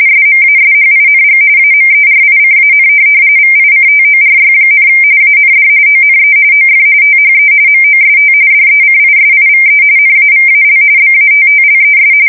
RTTY, about 45.45 baud. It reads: SMASHING AMERICAN VICTORY, AND ONE OF THE MOST DECISIVE NAVAL BATTLES
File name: fsk170 / USS-New-Jersey